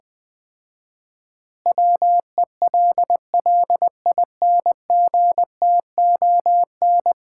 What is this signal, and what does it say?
Morse code, heard as a keyed tone, about 21 wpm, 700 Hz. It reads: WELLINGTON
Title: Common QSO Phrases - 4x Word-Spacing - 20wpm